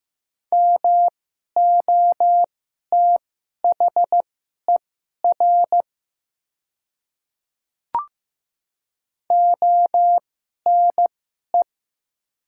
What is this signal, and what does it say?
Morse code, heard as a keyed tone, about 15 wpm, spaced as for 11 wpm, 700 Hz. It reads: MOTHER ONE
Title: 2x Character Spacing - Top 300 Words - 15wpm